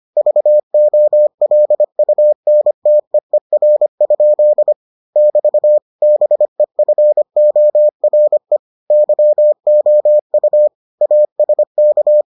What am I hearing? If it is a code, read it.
Morse code, heard as a keyed tone, about 25 wpm, 600 Hz. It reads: VOLUNTEER? <BT> BEFORE YOU ASK